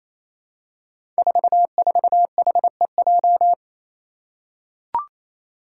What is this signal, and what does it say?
Morse code, heard as a keyed tone, about 28 wpm, 700 Hz. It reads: 44HEJ
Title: Five Challenging Letter-Number Combinations 28wpm